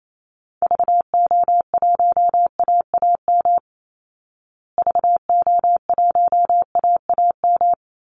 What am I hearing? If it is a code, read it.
Morse code, heard as a keyed tone, about 28 wpm, 700 Hz. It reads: VO1AAM VO1AAM
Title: Call Signs Repeated 2x - 28wpm